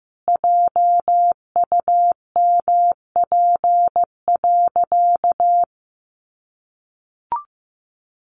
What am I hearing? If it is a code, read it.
Morse code, heard as a keyed tone, about 15 wpm, 700 Hz. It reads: JUMP.
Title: Sentences from Top 1300 Words 15wpm